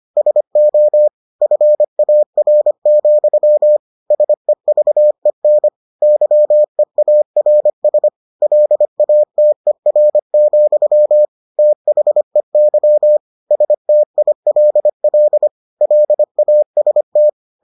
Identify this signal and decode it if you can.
Morse code, heard as a keyed tone, about 25 wpm, 600 Hz. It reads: SO FAR, SEVEN YEARS LATER, THEY STILL LAST